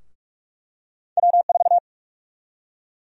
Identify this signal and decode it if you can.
Morse code, heard as a keyed tone, about 46 wpm, 700 Hz. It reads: W4
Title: USA Call Sign Prefixes - Repeated 2x - 45wpm